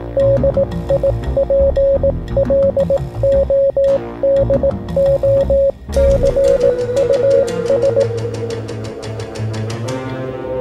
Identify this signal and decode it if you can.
Morse code, heard as a keyed tone, about 18 wpm, 550 Hz. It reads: DIPLODOCUS